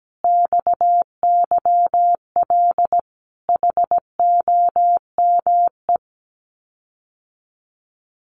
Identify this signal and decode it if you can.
Morse code, heard as a keyed tone, about 17 wpm, 700 Hz. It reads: XYL HOME
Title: Common QSO Phrases Encore 17wpm